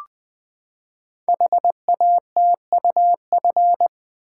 Morse code, heard as a keyed tone, about 20 wpm, 700 Hz. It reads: HATUF
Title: Code Groups - 5 Letters - 20wpm